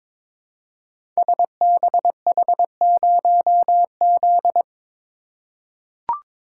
Morse code, heard as a keyed tone, about 22 wpm, 700 Hz. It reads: SBH0Z
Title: Code Groups - 5 Letter:Numbers - 22wpm